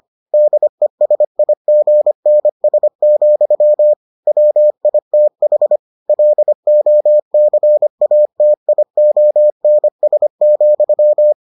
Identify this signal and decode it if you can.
Morse code, heard as a keyed tone, about 25 wpm, 600 Hz. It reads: DESIGNS, WITH LOCATIONS,